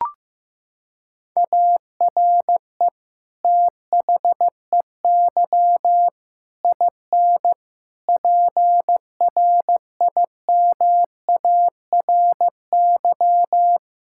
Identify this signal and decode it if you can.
Morse code, heard as a keyed tone, about 15 wpm, 700 Hz. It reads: ARE THEY IN PRIMARY
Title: Sentences from Top 1000 Words - Review Part 5 - 15wpm